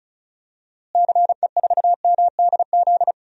Morse code, heard as a keyed tone, about 36 wpm, 700 Hz. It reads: CE4MDZ
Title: DXCC-100 Call Signs - 6 Characters - Repeated 3x - 35wpm